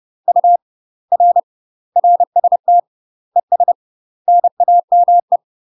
Morse code, heard as a keyed tone, about 30 wpm, 700 Hz. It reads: U R RST ES NAME
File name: UK MASTER QSOs TRAINING / 30WPM / UK MASTER QSOs 0043